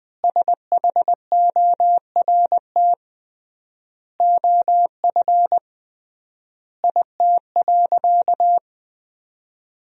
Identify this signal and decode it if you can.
Morse code, heard as a keyed tone, about 20 wpm, 700 Hz. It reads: SHORT OF IT.
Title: 3x Word-Spacing - Sentences from Top 400 Words - 20wpm